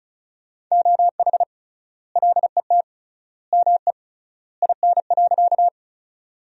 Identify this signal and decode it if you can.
Morse code, heard as a keyed tone, about 34 wpm, 700 Hz. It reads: OH LET ME IN.
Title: 3x Word-Spacing - Sentences from Top 500 Words - 35wpm